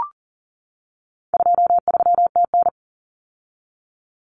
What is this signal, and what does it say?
Morse code, heard as a keyed tone, about 40 wpm, 700 Hz. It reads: 23TN
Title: Common Cut Numbers - 4-Digit 40wpm